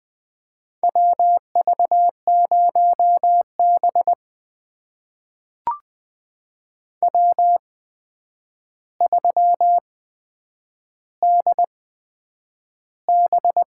Morse code, heard as a keyed tone, about 20 wpm, spaced as for 9 wpm, 700 Hz. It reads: WV0B W3DB
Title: USA Call Signs - 4 Characters - 8x Character Spacing - 20wpm